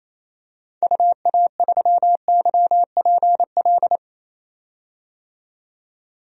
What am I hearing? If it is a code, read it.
Morse code, heard as a keyed tone, about 27 wpm, 700 Hz. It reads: UA3YPL
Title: DXCC-100 Call Signs - 6 Characters - 28wpm